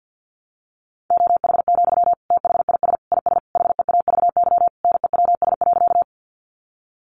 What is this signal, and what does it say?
Morse code, heard as a keyed tone, about 49 wpm, 700 Hz. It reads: OH, THIS IS HEAVY NEWS.
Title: Sentences from Top 1100 Words 50wpm